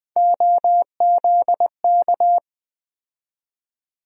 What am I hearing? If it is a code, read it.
Morse code, heard as a keyed tone, about 20 wpm, 700 Hz. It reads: OZK
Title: USA Call Sign Postfixes - 3 Letters 20wpm